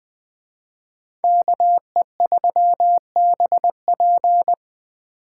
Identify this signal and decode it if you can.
Morse code, heard as a keyed tone, about 21 wpm, 700 Hz. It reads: KE3BP
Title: Speed Racing - USA 2x2 Call Signs 17wpm